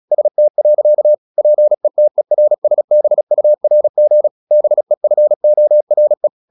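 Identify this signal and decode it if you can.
Morse code, heard as a keyed tone, about 36 wpm, 600 Hz. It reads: ST. PETERSBURG BEFORE
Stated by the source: Morse practice at 36 wpm, text of Ken Follett's novel Never, from CW med Gnister.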